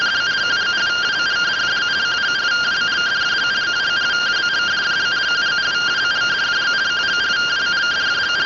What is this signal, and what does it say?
Two-tone teleprinter signal, about 170 Hz shift.
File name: findings / CT 205 ax mips, iDose 3 0007.gen-22 rtty rx